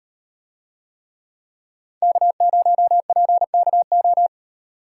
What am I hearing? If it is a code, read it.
Morse code, heard as a keyed tone, about 38 wpm, 700 Hz. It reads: K0PKO